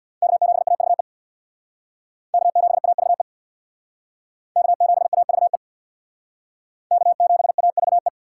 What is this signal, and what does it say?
Morse code, heard as a keyed tone, about 61 wpm, 700 Hz. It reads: K7AVE K7AVE K7AVE K7AVE
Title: Speed Racing - USA 1x3 Call Signs 50wpm